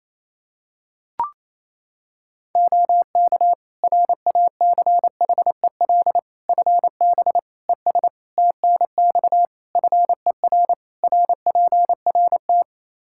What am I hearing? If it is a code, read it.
Morse code, heard as a keyed tone, about 28 wpm, 700 Hz. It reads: OK RACHEL FB ES TNX FER RPRT
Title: Practice QSO - Exchange 2 - Info - 28wpm